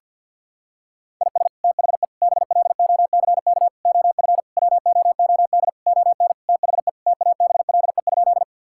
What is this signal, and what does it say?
Morse code, heard as a keyed tone, about 51 wpm, 700 Hz. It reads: IS THE BLOCK OF WOOD ON THE TABLE?
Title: Sentences from Top 1600 Words 50wpm